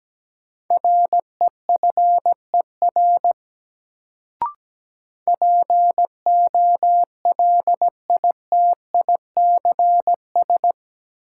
Morse code, heard as a keyed tone, about 17 wpm, 700 Hz. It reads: REFER POLITICS
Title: Sentences from Top 1000 Words - New Word Drill - 17wpm